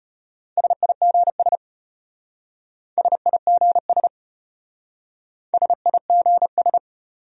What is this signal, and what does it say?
Morse code, heard as a keyed tone, about 34 wpm, 700 Hz. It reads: SIGS SIGS SIGS